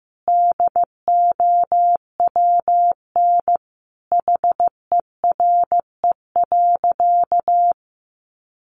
Morse code, heard as a keyed tone, about 15 wpm, 700 Hz. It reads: DOWN HERE.
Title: Sentences from Top 1100 Words 15wpm